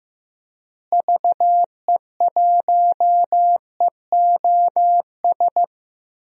Morse code, heard as a keyed tone, about 15 wpm, 700 Hz. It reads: VE1EOS
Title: DXCC-100 Call Signs - 6 Characters - Repeated 3x - 15wpm